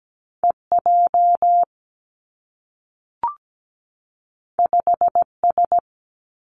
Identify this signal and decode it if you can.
Morse code, heard as a keyed tone, about 17 wpm, 700 Hz. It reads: EJ 5S
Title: Two Challenging Letter-Number Combinations 17wpm